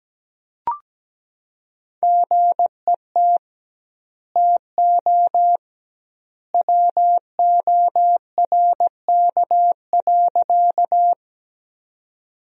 Morse code, heard as a keyed tone, about 17 wpm, 700 Hz. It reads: GET TO WORK.
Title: 2x Word-Spacing - Sentences from Top 200 Words - 17wpm